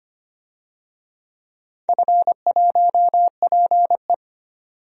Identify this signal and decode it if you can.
Morse code, heard as a keyed tone, about 25 wpm, 700 Hz. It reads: F1PE
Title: DXCC-100 Call Signs - 4 Characters - 25wpm